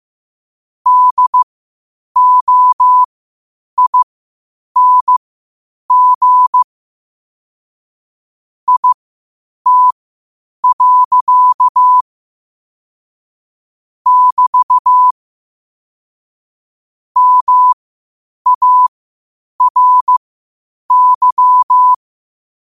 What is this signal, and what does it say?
Morse code, heard as a keyed tone, about 15 wpm, spaced as for 8 wpm, 1000 Hz. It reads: DOING IT. <BT> MARY
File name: qotdfiles / 8wpm